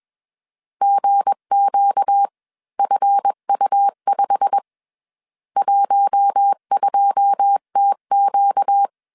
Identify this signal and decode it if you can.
Morse code, heard as a keyed tone, about 21 wpm, spaced as for 18 wpm, 800 Hz. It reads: GQFU5 12TQ